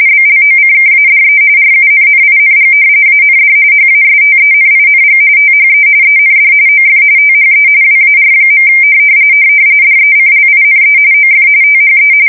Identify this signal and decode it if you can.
RTTY, about 45.45 baud. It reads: BLOCKADE RUNNING AND HASTENING THE END OF THE WAR. DECLINE OF THE
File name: fsk170 / usnavy3